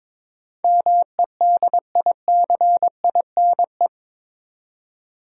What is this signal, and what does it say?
Morse code, heard as a keyed tone, about 22 wpm, 700 Hz. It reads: MEDICINE
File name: practice / 22wpm / Eight Letter Words - 22wpm